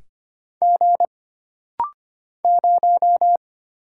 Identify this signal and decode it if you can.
Morse code, heard as a keyed tone, about 25 wpm, 700 Hz. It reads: G 0
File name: Single Character (Letter:Number:Punctuation:Prosign) - Rapid-Fire - 25wpm